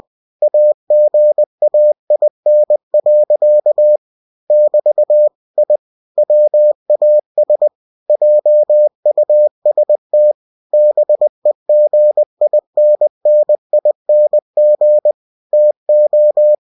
Morse code, heard as a keyed tone, about 20 wpm, 600 Hz. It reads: AGAIN. <BT> I WAS JUST BEGINNING TO
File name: cw-books / winnie-the-pooh / winnie-20wpm-page-0017